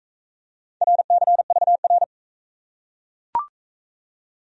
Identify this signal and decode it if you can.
Morse code, heard as a keyed tone, about 36 wpm, 700 Hz. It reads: RCVR